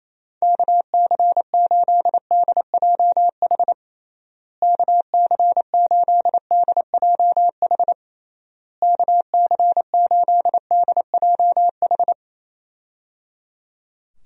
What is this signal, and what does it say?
Morse code, heard as a keyed tone, about 28 wpm, 700 Hz. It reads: KC8DJH KC8DJH KC8DJH
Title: USA Call Signs - 6 Characters - Repeated 3x - 28wpm